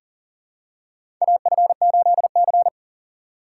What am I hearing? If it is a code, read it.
Morse code, heard as a keyed tone, about 40 wpm, 700 Hz. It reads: AF8C
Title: USA Call Signs - 4 Characters - 4x Character Spacing - 40wpm